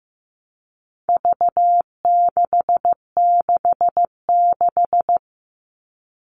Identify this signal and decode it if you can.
Morse code, heard as a keyed tone, about 15 wpm, 700 Hz. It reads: V666
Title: All Cut Numbers - 4-Digit 15wpm